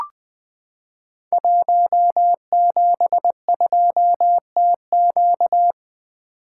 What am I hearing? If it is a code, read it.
Morse code, heard as a keyed tone, about 20 wpm, 700 Hz. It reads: 172TQ
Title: Code Groups - 5 Letter:Numbers - 20wpm